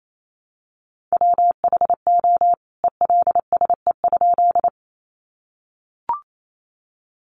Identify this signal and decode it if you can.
Morse code, heard as a keyed tone, about 28 wpm, 700 Hz. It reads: WHO ELSE?
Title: A Sound Of Thunder 28wpm